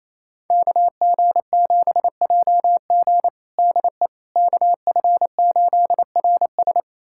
Morse code, heard as a keyed tone, about 28 wpm, 700 Hz. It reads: KG7JG DE KF8RS
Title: USA Callsign DE Callsign - 5 Characters - 28wpm